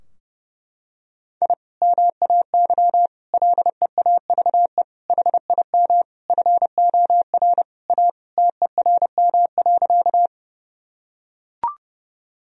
Morse code, heard as a keyed tone, about 32 wpm, 700 Hz. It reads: I MAY LEAVE HIM FOR A TERM.
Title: Sentences from Top 1000 Words - Review Part 4 - 30wpm